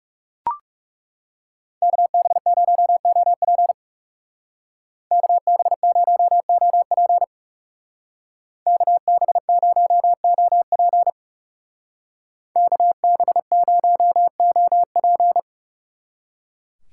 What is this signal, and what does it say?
Morse code, heard as a keyed tone, about 38 wpm, 690 Hz. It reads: KB0OP KB0OP KB0OP KB0OP